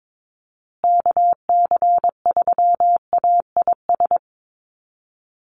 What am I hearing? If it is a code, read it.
Morse code, heard as a keyed tone, about 22 wpm, 700 Hz. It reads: KC3AIS
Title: USA Call Signs Repeated 3x 22wpm